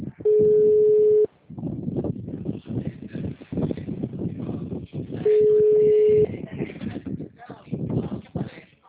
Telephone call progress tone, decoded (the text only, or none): ringback tone